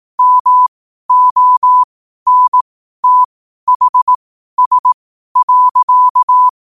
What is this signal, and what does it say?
Morse code, heard as a keyed tone, about 18 wpm, spaced as for 12 wpm, 1000 Hz. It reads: MONTHS.
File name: qotdfiles / 12wpm